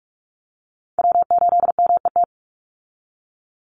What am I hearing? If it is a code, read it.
Morse code, heard as a keyed tone, about 44 wpm, 700 Hz. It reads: W8MET
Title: Call Signs Repeated 2x - 45wpm